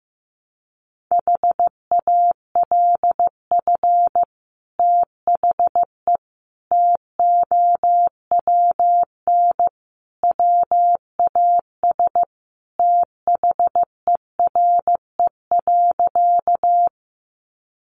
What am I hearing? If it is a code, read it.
Morse code, heard as a keyed tone, about 15 wpm, 700 Hz. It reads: HALF THE TOWN WAS THERE.
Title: Sentences from Top 500 Words 15wpm